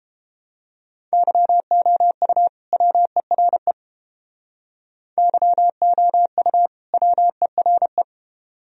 Morse code, heard as a keyed tone, about 28 wpm, 700 Hz. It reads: YOU WERE YOU WERE